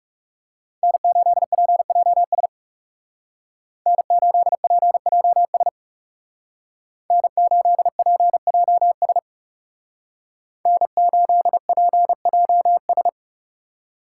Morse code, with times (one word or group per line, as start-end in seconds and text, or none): 0.8-2.5 N8PJS
3.9-5.7 N8PJS
7.1-9.2 N8PJS
10.7-13.1 N8PJS